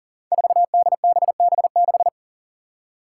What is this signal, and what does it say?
Morse code, heard as a keyed tone, about 40 wpm, 700 Hz. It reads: 4DBB6
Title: Five Challenging Letter-Number Combinations 40wpm